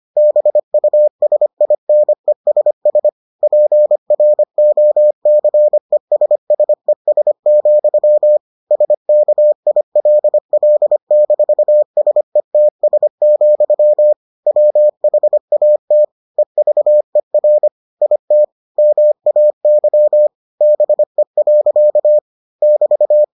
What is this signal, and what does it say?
Morse code, heard as a keyed tone, about 25 wpm, 600 Hz. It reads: BUSINESS PROCESSES, SKILL-SETS, WHAT EVER IT MAY BE. <BT>